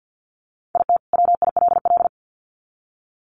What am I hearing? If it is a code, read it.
Morse code, heard as a keyed tone, about 50 wpm, 700 Hz. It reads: IT WILL
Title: Sets of 2 Words 50wpm